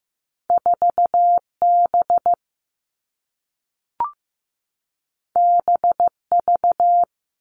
Morse code, heard as a keyed tone, about 15 wpm, 700 Hz. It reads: 4B BV
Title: All Cut Numbers - 2-Digit 15wpm